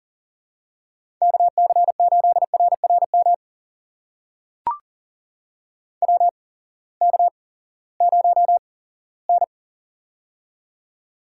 Morse code, heard as a keyed tone, about 38 wpm, spaced as for 20 wpm, 700 Hz. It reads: KC8RRM WK0N
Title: USA Call Signs - 8x Character Spacing - 40wpm